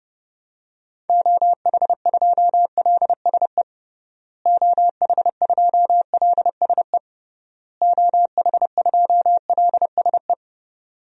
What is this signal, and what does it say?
Morse code, heard as a keyed tone, about 30 wpm, 700 Hz. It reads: OH2LSE OH2LSE OH2LSE
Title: DXCC-100 Call Signs - 6 Characters - Repeated 3x - 30wpm